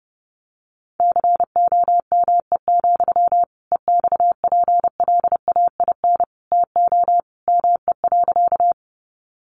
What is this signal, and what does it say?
Morse code, heard as a keyed tone, about 30 wpm, 700 Hz. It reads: COME, EXPLAIN TO ME.
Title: Sentences from Top 1100 Words 30wpm